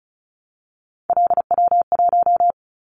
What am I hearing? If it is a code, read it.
Morse code, heard as a keyed tone, about 36 wpm, 700 Hz. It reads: LW1
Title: Callsign Prefixes for Top 100 DXCC - 35wpm